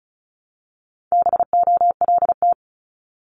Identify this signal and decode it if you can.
Morse code, heard as a keyed tone, about 36 wpm, 700 Hz. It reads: BOLT
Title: Multi-letter Phonemes 35wpm